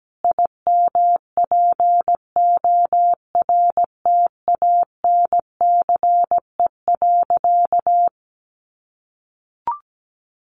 Morse code, heard as a keyed tone, about 17 wpm, 700 Hz. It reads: IMPORTANCE.
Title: Sentences from Top 1200 Words 17wpm